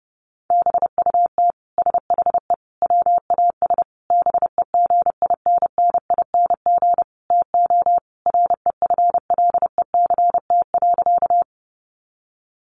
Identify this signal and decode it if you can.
Morse code, heard as a keyed tone, about 30 wpm, 700 Hz. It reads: BUT SHE WAS BEGINNING TO REFLECT.
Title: Sentences from Top 1500 Words 30wpm